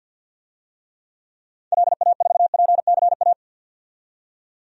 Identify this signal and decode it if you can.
Morse code, heard as a keyed tone, about 51 wpm, 700 Hz. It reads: LA4PCA
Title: DXCC-100 Call Signs - 6 Characters - 50wpm